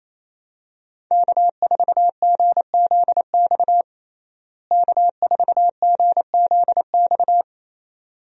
Morse code, heard as a keyed tone, about 28 wpm, 700 Hz. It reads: K4GZX K4GZX
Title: USA Call Signs - 5 Characters - Repeated 2x - 28wpm